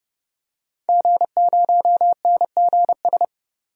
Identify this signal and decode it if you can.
Morse code, heard as a keyed tone, about 30 wpm, 700 Hz. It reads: G0NGS
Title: DXCC-100 Call Signs - 5 Characters - 30wpm